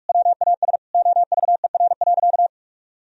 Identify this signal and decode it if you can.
Morse code, heard as a keyed tone, about 46 wpm, 700 Hz. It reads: WAS OVER.
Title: Sentences from Top 1000 Words - Review Part 1 - 45wpm